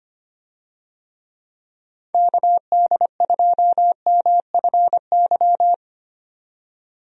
Morse code, heard as a keyed tone, about 25 wpm, 700 Hz. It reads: KD2MFY
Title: USA Call Signs - 6 Characters - 25wpm